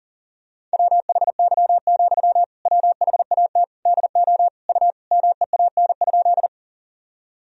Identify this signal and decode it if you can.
Morse code, heard as a keyed tone, about 40 wpm, 700 Hz. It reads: WHY, WHAT DO U MEAN?
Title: Sentences from Top 200 Words and Abbreviations 40wpm